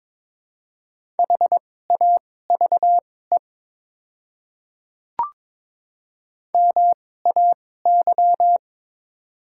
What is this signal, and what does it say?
Morse code, heard as a keyed tone, about 22 wpm, spaced as for 16 wpm, 690 Hz. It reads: HAVE MAY
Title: 2x Character Spacing - Top 100 Words - 22wpm